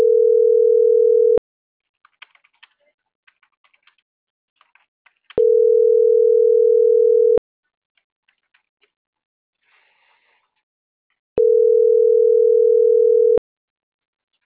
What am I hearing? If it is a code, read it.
Ringback tone (North America), 440 + 480 Hz.